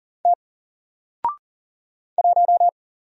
Morse code, heard as a keyed tone, about 39 wpm, 700 Hz. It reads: T 1
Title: Single Character (Letter:Number:Punctuation:Prosign) 40wpm